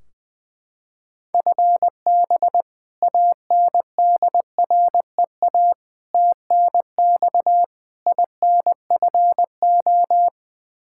Morse code, heard as a keyed tone, about 20 wpm, 700 Hz. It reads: FB ANDREA TNX INFO
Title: Practice QSO - Exchange 3 - More Info - 20wpm